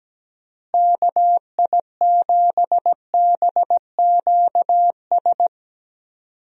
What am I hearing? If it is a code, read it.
Morse code, heard as a keyed tone, about 17 wpm, 700 Hz. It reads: KI7BQS
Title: USA Call Signs - 6 Characters - Repeated 3x - 17wpm